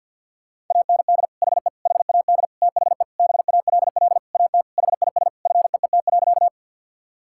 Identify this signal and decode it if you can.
Morse code, heard as a keyed tone, about 49 wpm, 700 Hz. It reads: AND HE HAD THE BALL AT HIS FEET.
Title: Sentences from Top 1000 Words - Review Part 6 - 50wpm